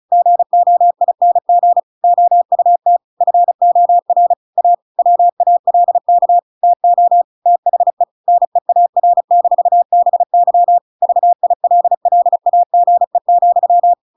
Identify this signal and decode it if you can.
Morse code, heard as a keyed tone, about 34 wpm, 700 Hz. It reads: GOING OUT FOR A WALK TO THE NEAR-BY VILLAGE,